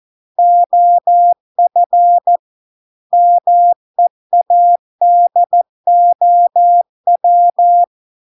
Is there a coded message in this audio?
Yes: Morse code from 0.4 to 7.8 s, heard as a keyed tone, about 14 wpm, 700 Hz. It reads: OF MEADOW